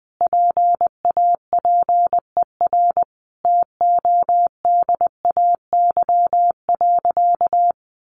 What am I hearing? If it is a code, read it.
Morse code, heard as a keyed tone, about 20 wpm, 700 Hz. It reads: PAPER TODAY.
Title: Sentences from Top 600 Words 20wpm